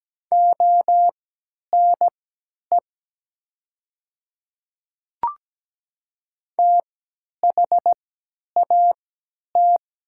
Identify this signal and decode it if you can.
Morse code, heard as a keyed tone, about 17 wpm, spaced as for 10 wpm, 700 Hz. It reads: ONE THAT
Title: 3x Character Spacing - Top 100 Words - 17wpm